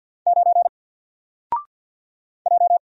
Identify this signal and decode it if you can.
Morse code, heard as a keyed tone, about 48 wpm, 700 Hz. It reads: 9 J
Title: Single Letter-Number Alt 50wpm